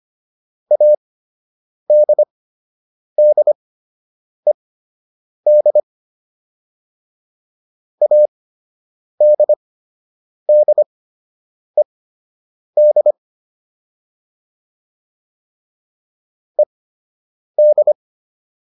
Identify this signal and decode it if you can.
Morse code, heard as a keyed tone, about 25 wpm, spaced as for 8 wpm, 600 Hz. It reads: ADDED ADDED ED